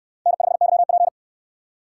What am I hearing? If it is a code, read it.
Morse code, heard as a keyed tone, about 69 wpm, 700 Hz. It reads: N5YP